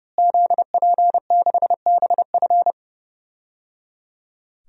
Morse code, heard as a keyed tone, about 30 wpm, 700 Hz. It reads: ZP6BF
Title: DXCC-100 Call Signs - 5 Characters - Repeated 3x - 30wpm